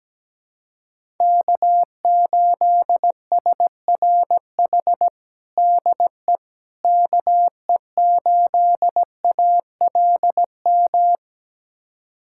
Morse code, heard as a keyed tone, about 17 wpm, 700 Hz. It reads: K8SRH DE KE8ALM
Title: USA Callsign DE Callsign 17wpm